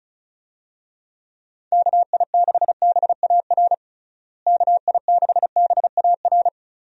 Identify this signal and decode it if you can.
Morse code, heard as a keyed tone, about 36 wpm, 700 Hz. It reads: KI6BAR KI6BAR
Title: USA Call Signs - 6 Characters - Repeated 2x - 35wpm